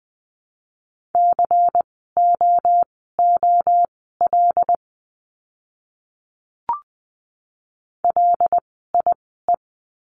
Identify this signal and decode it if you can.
Morse code, heard as a keyed tone, about 20 wpm, spaced as for 14 wpm, 700 Hz. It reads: COOL LIE
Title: 094 Lesson 16f - Letter c - Words 20wpm